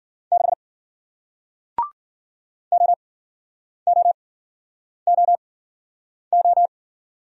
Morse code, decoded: 6 OOOO

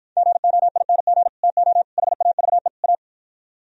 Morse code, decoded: GOING TO HAVE A

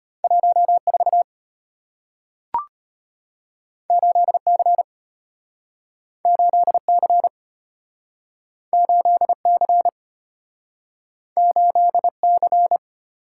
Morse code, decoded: 14 8C 8C 8C 8C

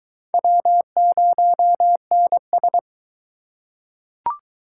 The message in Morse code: W0NS

